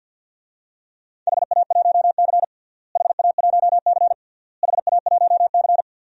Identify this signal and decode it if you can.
Morse code, heard as a keyed tone, about 48 wpm, 700 Hz. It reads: HA1C HA1C HA1C
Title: DXCC-100 Call Signs - 4 Characters - Repeated 3x - 50wpm